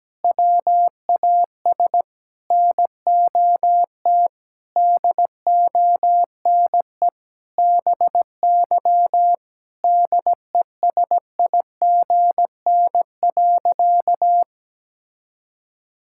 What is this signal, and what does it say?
Morse code, heard as a keyed tone, about 17 wpm, 700 Hz. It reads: WAS NOT DONE BY DESIGN.
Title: Sentences from Top 1000 Words - Review Part 5 - 17wpm